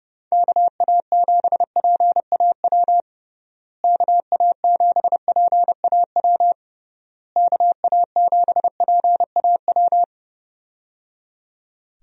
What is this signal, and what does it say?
Morse code, heard as a keyed tone, about 32 wpm, 700 Hz. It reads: KA7PAW KA7PAW KA7PAW
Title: USA Call Signs - 6 Characters - Repeated 3x - 30wpm